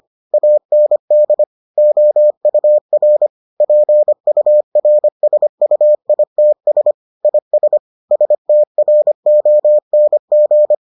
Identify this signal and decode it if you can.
Morse code, heard as a keyed tone, about 25 wpm, 600 Hz. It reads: AND OUR PURSUITS IS STRONG